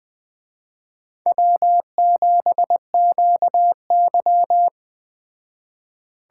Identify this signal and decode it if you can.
Morse code, heard as a keyed tone, about 20 wpm, 700 Hz. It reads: W7QY
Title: Speed Racing - USA 1x2 Call Signs 17wpm